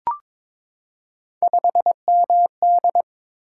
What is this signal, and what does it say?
Morse code, heard as a keyed tone, about 22 wpm, 700 Hz. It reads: 5MD
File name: practice / 22wpm / Three Letter-Number Combinations - 22wpm